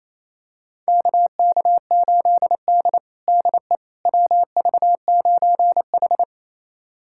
Morse code, heard as a keyed tone, about 27 wpm, 700 Hz. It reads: KK8D DE WV9H
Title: USA Callsign DE Callsign - 4 Characters - 28wpm